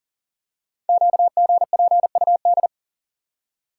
Morse code, heard as a keyed tone, about 39 wpm, 690 Hz. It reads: QGPUD
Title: Code Groups - 5 Letters - 40wpm